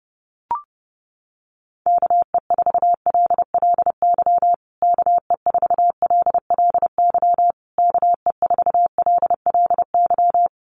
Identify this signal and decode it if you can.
Morse code, heard as a keyed tone, about 30 wpm, 700 Hz. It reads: KE4LLY KE4LLY KE4LLY
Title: USA Call Signs Repeated 3x 30wpm